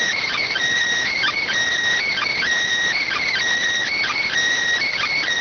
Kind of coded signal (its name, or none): Emergency Alert System (SAME) tones